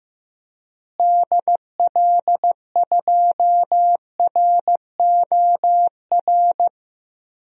Morse code, heard as a keyed tone, about 15 wpm, 700 Hz. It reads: DL2ROR
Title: DXCC-100 Call Signs - 6 Characters - Repeated 2x - 15wpm